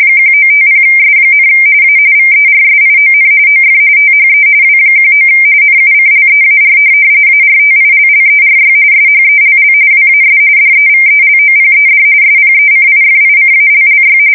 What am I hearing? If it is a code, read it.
RTTY, about 45.45 baud. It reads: 1950S, A VERSION OF THE W19 NUCLEAR ARTILLERY SHELL WAS DEVELOPED SPECIFICALLY
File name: fsk170 / USS-New-Jersey